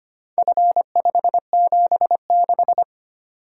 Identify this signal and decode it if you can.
Morse code, heard as a keyed tone, about 25 wpm, 700 Hz. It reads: F576
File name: practice / 25wpm / Four Letter-Number Combinations - 25wpm